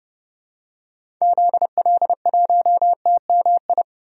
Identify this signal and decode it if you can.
Morse code, heard as a keyed tone, about 30 wpm, 700 Hz. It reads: ZL1TMI
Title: DXCC-100 Call Signs - 6 Characters - 30wpm